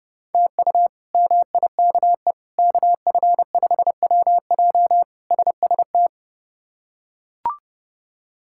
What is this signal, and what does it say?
Morse code, heard as a keyed tone, about 32 wpm, 700 Hz. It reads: TU MIKE KF5WJ SST
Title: K1USN SST QSO Practice - 30wpm